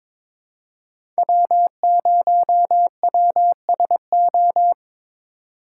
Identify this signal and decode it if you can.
Morse code, heard as a keyed tone, about 22 wpm, 700 Hz. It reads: W0WSO
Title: USA Call Signs - 5 Characters - 3x Character Spacing - 22wpm